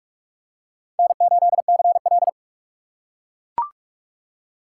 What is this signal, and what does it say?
Morse code, heard as a keyed tone, about 41 wpm, 700 Hz. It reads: N8CL